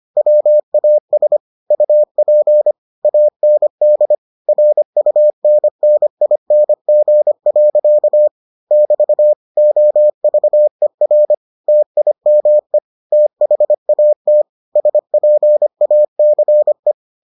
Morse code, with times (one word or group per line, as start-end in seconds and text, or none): 0.2-1.4 WAS
1.7-2.7 UP
3.0-4.1 AND
4.5-8.3 RUNNING.
8.7-9.3 <BT>
9.6-11.3 OVER
11.7-12.8 TIME
13.1-14.4 THAT
14.8-16.9 SPACE